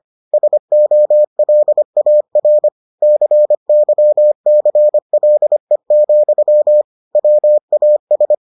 Morse code, heard as a keyed tone, about 25 wpm, 600 Hz. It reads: SOLAR CYCLE, WAS